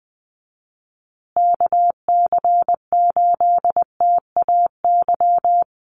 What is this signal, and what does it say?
Morse code, heard as a keyed tone, about 20 wpm, 700 Hz. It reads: KC8TAY
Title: USA Call Signs Repeated 2x 20wpm